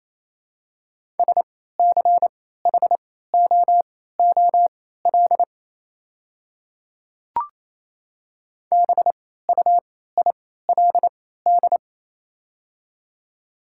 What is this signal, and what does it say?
Morse code, heard as a keyed tone, about 28 wpm, spaced as for 16 wpm, 700 Hz. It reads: SCHOOL BUILD
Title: 3x Character Spacing - Top 300 Words - 28wpm